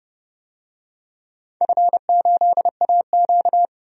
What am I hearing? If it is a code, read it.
Morse code, heard as a keyed tone, about 30 wpm, 700 Hz. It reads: F8AQ
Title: DXCC-100 Call Signs - 4 Characters - 30wpm